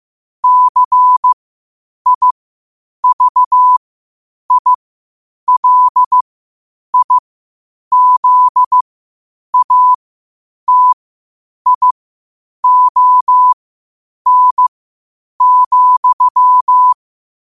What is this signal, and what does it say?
Morse code, heard as a keyed tone, about 15 wpm, spaced as for 8 wpm, 1000 Hz. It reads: CIVILIZATION,
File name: qotdfiles / 8wpm